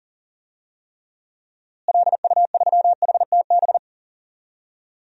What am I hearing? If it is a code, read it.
Morse code, heard as a keyed tone, about 37 wpm, 700 Hz. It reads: LU3HTB
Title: DXCC-100 Call Signs - 6 Characters - Repeated 3x - 40wpm